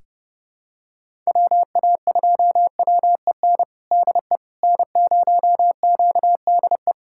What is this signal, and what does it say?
Morse code, heard as a keyed tone, about 32 wpm, 700 Hz. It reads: WA2WEN DE N0QDE
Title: USA Callsign DE Callsign 30wpm